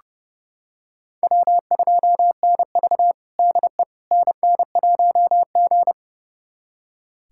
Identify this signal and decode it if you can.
Morse code, heard as a keyed tone, about 30 wpm, 700 Hz. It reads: W2NV DE NN1G
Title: USA Callsign DE Callsign - 4 Characters - 30wpm